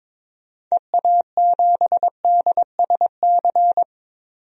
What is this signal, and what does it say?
Morse code, heard as a keyed tone, about 22 wpm, 700 Hz. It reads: EA7DSC
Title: DXCC-100 Call Signs - 6 Characters - 22wpm